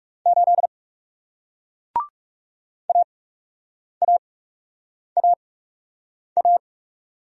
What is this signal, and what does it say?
Morse code, heard as a keyed tone, about 40 wpm, spaced as for 8 wpm, 700 Hz. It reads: 8 AAAA